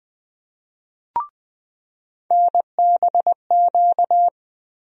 Morse code, heard as a keyed tone, about 20 wpm, 700 Hz. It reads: NBQ
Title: USA Call Sign Postfixes - 3 Letters 20wpm